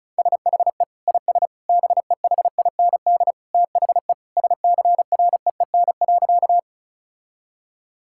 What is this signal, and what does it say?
Morse code, heard as a keyed tone, about 34 wpm, 700 Hz. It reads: SHE IS BEHIND THE SCREEN.
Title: Sentences from Top 1000 Words - Review Part 6 - 35wpm